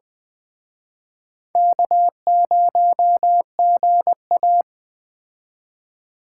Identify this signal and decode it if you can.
Morse code, heard as a keyed tone, about 20 wpm, 700 Hz. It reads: K0GA